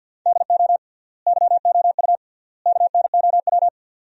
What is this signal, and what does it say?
Morse code, heard as a keyed tone, about 50 wpm, 700 Hz. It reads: DO YOU KNOW
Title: 3x Word-Spacing - Sentences from Top 100 Words - 50wpm